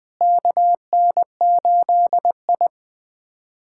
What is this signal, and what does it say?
Morse code, heard as a keyed tone, about 20 wpm, 700 Hz. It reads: KN8I
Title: USA Callsign DE Callsign - 4 Characters - 20wpm